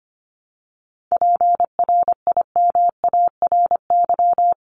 Morse code, heard as a keyed tone, about 25 wpm, 700 Hz. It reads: PRIMARY
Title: LY Suffix - Top 100 Words with Base Words 25wpm